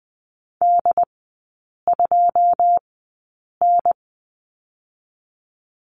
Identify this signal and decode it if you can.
Morse code, heard as a keyed tone, about 20 wpm, spaced as for 8 wpm, 690 Hz. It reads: D2N
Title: 081 Lesson 14e - Number 2 - Three Characters 20wpm